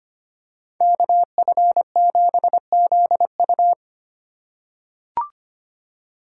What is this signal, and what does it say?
Morse code, heard as a keyed tone, about 25 wpm, 700 Hz. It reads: KF7ZU
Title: USA Call Signs - 5 Characters - 2x Character Spacing - 25wpm